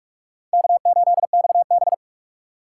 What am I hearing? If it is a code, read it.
Morse code, heard as a keyed tone, about 38 wpm, 700 Hz. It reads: K8XB